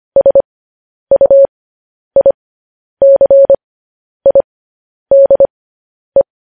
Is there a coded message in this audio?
Yes: Morse code from 0.2 to 6.2 s, heard as a keyed tone, about 25 wpm, spaced as for 10 wpm, 550 Hz. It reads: SUICIDE